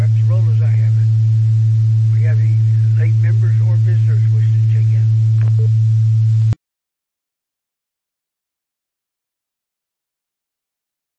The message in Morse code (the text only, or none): none